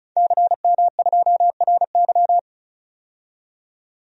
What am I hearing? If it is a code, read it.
Morse code, heard as a keyed tone, about 36 wpm, 700 Hz. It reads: CM2RY
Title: DXCC-100 Call Signs - 5 Characters - 35wpm